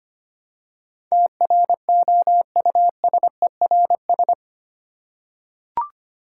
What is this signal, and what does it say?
Morse code, heard as a keyed tone, about 25 wpm, 700 Hz. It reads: TROUSERS
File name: Eight Letter Words - 25wpm